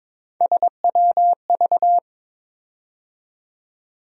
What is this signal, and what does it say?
Morse code, heard as a keyed tone, about 22 wpm, 700 Hz. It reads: SWV
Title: USA Call Sign Postfixes - 3 Letters 22wpm